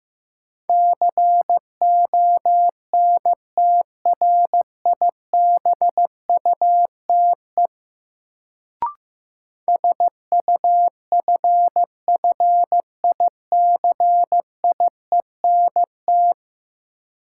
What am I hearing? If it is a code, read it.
Morse code, heard as a keyed tone, about 15 wpm, 700 Hz. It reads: CONTRIBUTE SUFFICIENT
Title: Sentences from Top 1400 Words - New Word Drill - 15wpm